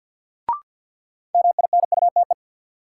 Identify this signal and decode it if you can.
Morse code, heard as a keyed tone, about 53 wpm, 700 Hz. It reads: MINUTE
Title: Sentences from Top 500 Words - New Word Drill - 50wpm